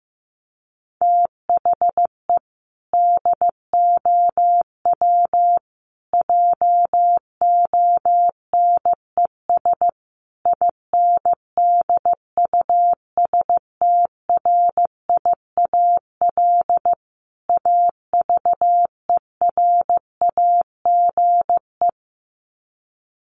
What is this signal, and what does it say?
Morse code, heard as a keyed tone, about 15 wpm, 700 Hz. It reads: THE DOW JONES INDUSTRIAL AVERAGE
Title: Sets of 5 Words 15wpm